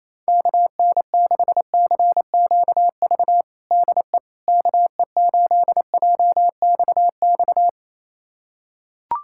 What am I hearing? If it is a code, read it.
Morse code, heard as a keyed tone, about 28 wpm, 700 Hz. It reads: KN6CQV DE KE8JXX
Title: USA Callsign DE Callsign - 6 Characters - 28wpm